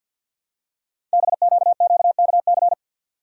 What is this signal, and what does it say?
Morse code, heard as a keyed tone, about 52 wpm, 700 Hz. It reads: BQQKC
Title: Code Groups - 5 Letters - 50wpm